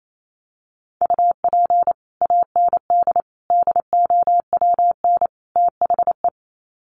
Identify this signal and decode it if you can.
Morse code, heard as a keyed tone, about 28 wpm, 700 Hz. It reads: UP AND DOWN THE
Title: Sets of 4 Words 28wpm